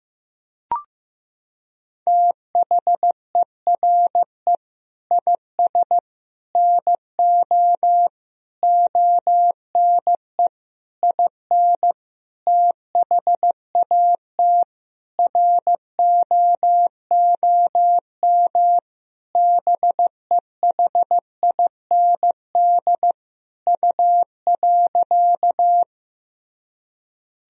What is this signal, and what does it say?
Morse code, heard as a keyed tone, about 15 wpm, 700 Hz. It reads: THERE IS NO ONE IN THAT ROOM BEHIND U.
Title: Sentences from Top 500 Words and Abbreviations 15wpm